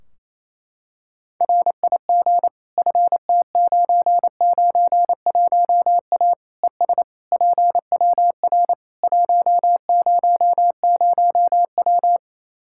Morse code, heard as a keyed tone, about 28 wpm, 700 Hz. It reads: RIG FT991A ES PWR 100W
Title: K1USN SST QSO Practice 28wpm